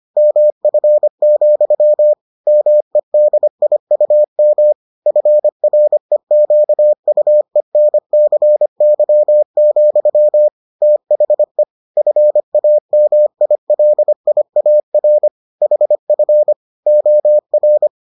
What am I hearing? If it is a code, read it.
Morse code, heard as a keyed tone, about 25 wpm, 600 Hz. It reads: MF, MEDIUM FREQUENCY, THE FAMILIAR HF OR